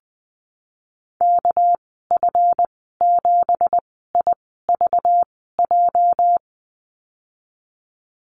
Morse code, heard as a keyed tone, about 20 wpm, spaced as for 15 wpm, 700 Hz. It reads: KF7IVJ
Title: 213 Lesson 34g - Number 8 - Call Signs 20wpm